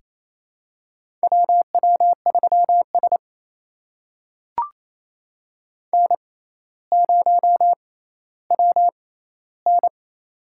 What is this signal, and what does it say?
Morse code, heard as a keyed tone, about 28 wpm, spaced as for 14 wpm, 700 Hz. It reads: WW3S N0WN
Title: USA Call Signs - 4 Characters - 6x Character Spacing - 28wpm